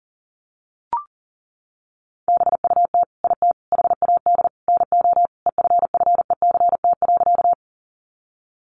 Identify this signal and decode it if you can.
Morse code, heard as a keyed tone, about 40 wpm, 700 Hz. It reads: BUT IT HAD NO EFFECT.
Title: Sentences from Top 1000 Words 40wpm